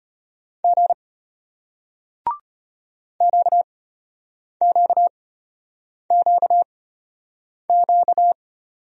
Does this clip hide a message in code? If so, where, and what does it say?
Yes: Morse code from 0.6 to 8.3 s, heard as a keyed tone, about 34 wpm, spaced as for 8 wpm, 700 Hz. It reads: G QQQQ